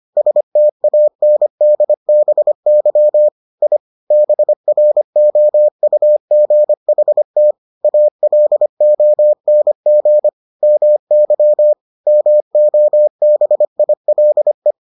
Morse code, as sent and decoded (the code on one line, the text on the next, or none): ... - .- -. -.. -... -.-- / .. / -... .-. --- ..- --. .... - / .- .-.. --- -. --. / -- -.-- / -- --- -... .. .-.. .
STANDBY I BROUGHT ALONG MY MOBILE